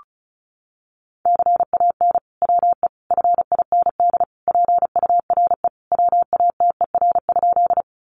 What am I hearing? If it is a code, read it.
Morse code, heard as a keyed tone, about 34 wpm, 700 Hz. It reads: CAN WE FIND PURE WATER?
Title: Sentences from Top 1500 Words 35wpm